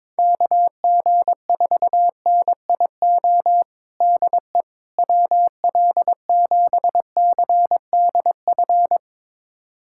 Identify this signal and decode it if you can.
Morse code, heard as a keyed tone, about 22 wpm, 700 Hz. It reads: KG4NIO DE WL7CDF
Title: USA Callsign DE Callsign - 6 Characters - 22wpm